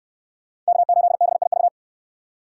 Morse code, heard as a keyed tone, about 64 wpm, 700 Hz. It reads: K8BIV